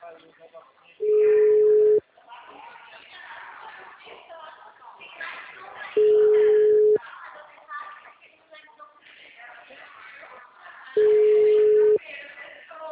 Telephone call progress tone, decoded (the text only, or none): ringback tone